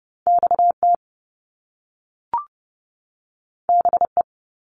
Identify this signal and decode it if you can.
Morse code, heard as a keyed tone, about 30 wpm, 700 Hz. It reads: XT BE
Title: Top Two Letter Combinations in Words 30wpm